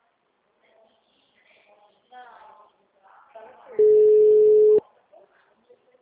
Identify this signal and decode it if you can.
Ringback tone (Europe, 425 Hz), with speech.